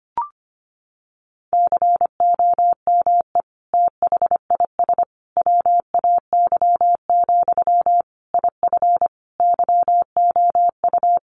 Morse code, heard as a keyed tone, about 25 wpm, 700 Hz. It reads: COME THIS WAY, IF YOU
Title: Sentences from Top 1500 Words 25wpm